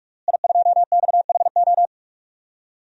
Morse code, heard as a keyed tone, about 46 wpm, 700 Hz. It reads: I2XHO
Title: DXCC-100 Call Signs - 5 Characters - Repeated 3x - 45wpm